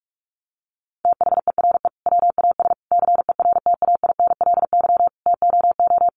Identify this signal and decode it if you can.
Morse code, heard as a keyed tone, about 44 wpm, 700 Hz. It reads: THERE WAS CERTAINLY TOO
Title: Sentences from Top 900 Words 45wpm